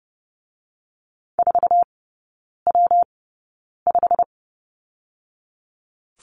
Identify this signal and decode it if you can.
Morse code, heard as a keyed tone, about 30 wpm, 700 Hz. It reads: 4 W 5
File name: Lesson 19e - Letter w - Three Characters 30wpm